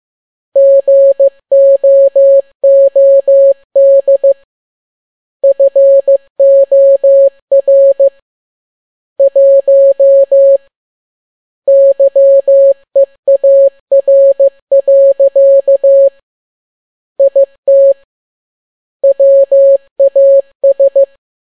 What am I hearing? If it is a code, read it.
Morse code, heard as a keyed tone, about 15 wpm, 550 Hz. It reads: GOOD FOR 1 YEAR. IT WAS
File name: n6tt p1 15C13